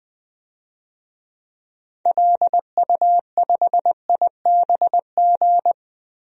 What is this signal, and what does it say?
Morse code, heard as a keyed tone, about 21 wpm, 700 Hz. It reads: LU5IBG
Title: DXCC-100 Call Signs - 6 Characters - 20wpm